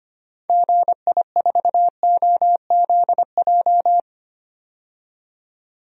Morse code, heard as a keyed tone, about 25 wpm, 700 Hz. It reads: GI4OZJ
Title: DXCC-100 Call Signs - 6 Characters - 25wpm